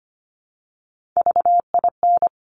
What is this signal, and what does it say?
Morse code, heard as a keyed tone, about 25 wpm, 700 Hz. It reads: VIN
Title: Top Three Letter Combinations in Words 25wpm